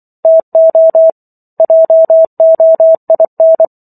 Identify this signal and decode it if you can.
Morse code, heard as a keyed tone, about 24 wpm, 650 Hz. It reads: TO JOIN